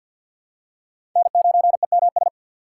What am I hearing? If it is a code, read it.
Morse code, heard as a keyed tone, about 52 wpm, 700 Hz. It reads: N9EMS
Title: USA Call Signs - 4x Character Spacing - 50wpm